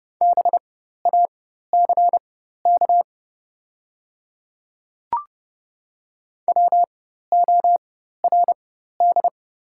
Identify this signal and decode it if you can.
Morse code, heard as a keyed tone, about 30 wpm, spaced as for 14 wpm, 700 Hz. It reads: BACK WORD
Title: 4x Character Spacing - Top 200 Words - 30wpm